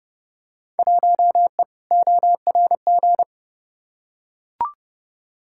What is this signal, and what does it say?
Morse code, heard as a keyed tone, about 32 wpm, 700 Hz. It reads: 1E ORG
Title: Field Day Exchanges - 30wpm